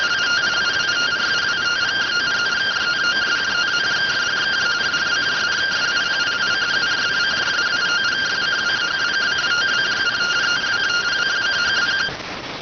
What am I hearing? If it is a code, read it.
Two-tone teleprinter signal, about 170 Hz shift.